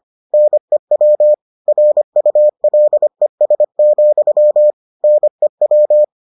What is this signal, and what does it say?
Morse code, heard as a keyed tone, about 25 wpm, 600 Hz. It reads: NEW RULES, NEW